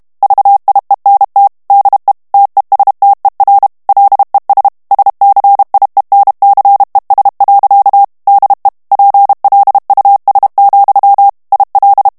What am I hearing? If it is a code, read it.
Morse code, heard as a keyed tone, about 32 wpm, 800 Hz. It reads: VIENT DE TESTER LES SCIENCES. DE PLUS, IL
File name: morse mp3 / FRANCAIS TEXTE CLAIR 1 VITESSE 30 MOTS MINUTES